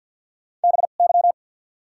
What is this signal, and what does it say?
Morse code, heard as a keyed tone, about 53 wpm, 700 Hz. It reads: B Y
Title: 2x Character Spacing - Top 300 Words - 50wpm